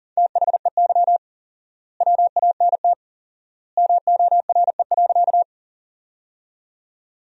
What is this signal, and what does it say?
Morse code, heard as a keyed tone, about 40 wpm, 700 Hz. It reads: THEY WANT MORE.
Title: 4x Word-Spacing - Sentences from Top 200 Words - 40wpm